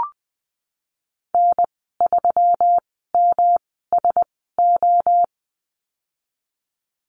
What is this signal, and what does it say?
Morse code, heard as a keyed tone, about 19 wpm, spaced as for 14 wpm, 700 Hz. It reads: N3MSO
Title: 161 Lesson 26g - Letter g - Call Signs 20wpm